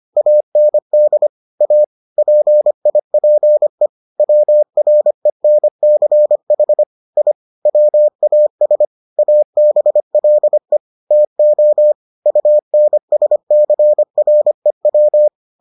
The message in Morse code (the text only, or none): AND A PIPE WRENCH I WAS ABLE TO UNSCREW